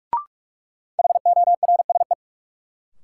Morse code, heard as a keyed tone, about 42 wpm, 700 Hz. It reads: HORSE
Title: Sentences from Top 300 Words - New Word Drill - 45wpm